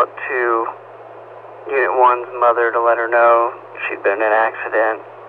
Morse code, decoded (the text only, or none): none